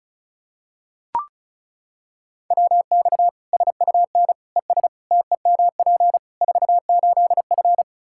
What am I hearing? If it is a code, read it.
Morse code, heard as a keyed tone, about 36 wpm, 700 Hz. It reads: WX SUN ES TEMP 48F
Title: K1USN SST QSO Practice 35wpm